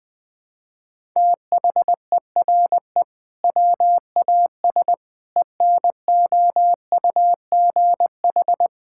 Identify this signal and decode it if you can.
Morse code, heard as a keyed tone, about 19 wpm, 700 Hz. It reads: THERE WAS ENOUGH
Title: Sentences from Top 1000 Words - Review Part 6 - 20wpm